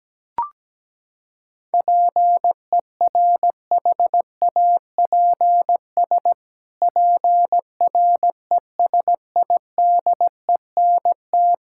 Morse code, heard as a keyed tone, about 17 wpm, 700 Hz. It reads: PERHAPS PRESIDENT
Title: Pangram Challenge v2 - 17wpm